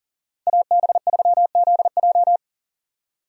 Morse code, heard as a keyed tone, about 40 wpm, 700 Hz. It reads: AB3ZJ
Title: USA Call Signs - 5 Characters - 40wpm